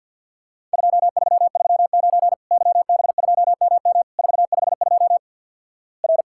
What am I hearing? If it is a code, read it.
Morse code, heard as a keyed tone, about 50 wpm, 700 Hz. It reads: 2339 YB2MM 452 R
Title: Worked All Europe DX Contest - QTC Exchange with Repeats - 50wpm